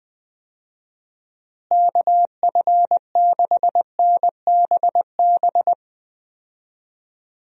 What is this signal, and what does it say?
Morse code, heard as a keyed tone, about 20 wpm, 700 Hz. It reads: KF6NBB
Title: USA Call Signs - 6 Characters - Repeated 3x - 20wpm